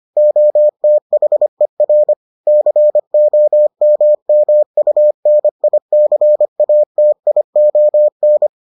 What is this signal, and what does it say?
Morse code, heard as a keyed tone, about 25 wpm, 600 Hz. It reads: OTHER COMMUNICATION